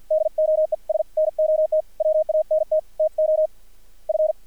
Morse code, heard as a keyed tone, about 50 wpm, 620 Hz. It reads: DOES NOT WANT TO F
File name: archived / CRSnet